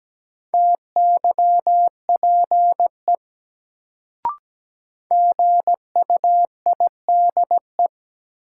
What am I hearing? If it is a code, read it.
Morse code, heard as a keyed tone, about 17 wpm, 700 Hz. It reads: TYPE GUIDE
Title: Sentences from Top 600 Words - New Word Drill - 17wpm